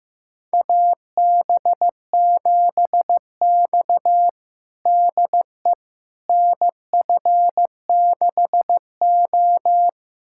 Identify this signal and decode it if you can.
Morse code, heard as a keyed tone, about 15 wpm, 700 Hz. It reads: AB7X DE NF6O
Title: USA Callsign DE Callsign - 4 Characters - 15wpm